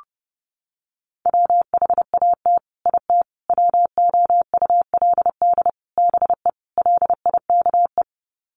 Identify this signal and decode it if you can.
Morse code, heard as a keyed tone, about 29 wpm, 700 Hz. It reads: WHAT IT WOULD BE LIKE
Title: Sets of 5 Words Encore 30wpm